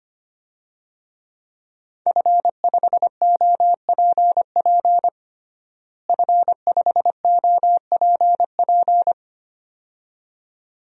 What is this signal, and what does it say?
Morse code, heard as a keyed tone, about 25 wpm, 700 Hz. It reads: F5OPP F5OPP
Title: DXCC-100 Call Signs - 5 Characters - Repeated 2x - 25wpm